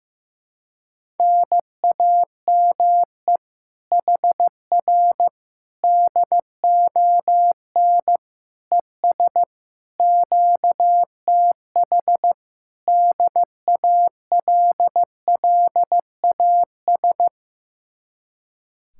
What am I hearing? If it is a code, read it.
Morse code, heard as a keyed tone, about 15 wpm, 700 Hz. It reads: NAME HR DON ES QTH DALLAS
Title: Speed-Racing Common QSO Phrases Encore - 15wpm